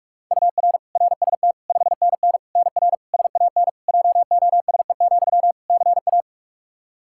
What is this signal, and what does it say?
Morse code, heard as a keyed tone, about 41 wpm, 700 Hz. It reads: UR RST 5NN NR SAN JOSE, CA
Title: Practice QSO - Exchange 1 - Beginning - 45wpm